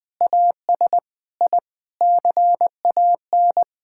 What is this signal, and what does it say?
Morse code, heard as a keyed tone, about 21 wpm, 700 Hz. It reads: AS I CAN
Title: Sentences with 8-Word Phrases - 20wpm